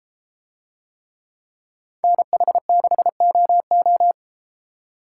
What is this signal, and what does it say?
Morse code, heard as a keyed tone, about 32 wpm, 700 Hz. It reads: NH6OO